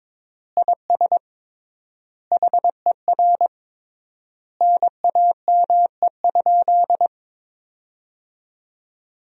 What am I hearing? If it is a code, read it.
Morse code, heard as a keyed tone, about 22 wpm, 700 Hz. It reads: IS HER NAME?
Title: 3x Word-Spacing - Sentences from Top 200 Words - 22wpm